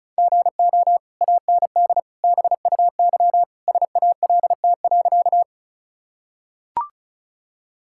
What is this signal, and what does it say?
Morse code, heard as a keyed tone, about 34 wpm, 700 Hz. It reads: GO AND BUY SALT.
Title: Sentences from Top 1000 Words - Review Part 5 - 35wpm